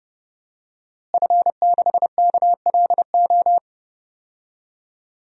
Morse code, heard as a keyed tone, about 30 wpm, 700 Hz. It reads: F6KLO
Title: DXCC-100 Call Signs - 5 Characters - Repeated 3x - 30wpm